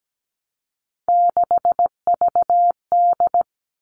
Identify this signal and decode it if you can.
Morse code, heard as a keyed tone, about 17 wpm, 700 Hz. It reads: 6VD
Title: All Cut Numbers - 3-Digit 17wpm